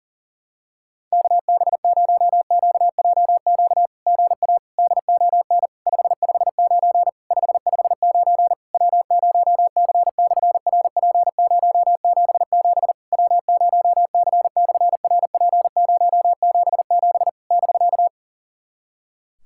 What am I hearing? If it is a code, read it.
Morse code, heard as a keyed tone, about 40 wpm, 700 Hz. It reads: KB0QJQ GA DON 559 559 W0C/RP077 W0C/RP077 <BK>
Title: SOTA QSO Practice 40wpm